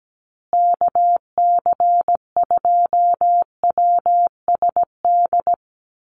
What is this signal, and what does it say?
Morse code, heard as a keyed tone, about 17 wpm, 700 Hz. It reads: KC2WSD
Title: USA Call Signs Repeated 3x 17wpm